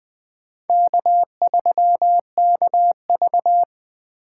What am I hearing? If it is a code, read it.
Morse code, heard as a keyed tone, about 21 wpm, 700 Hz. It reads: K3KV
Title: USA Call Signs - 2x Character Spacing - 20wpm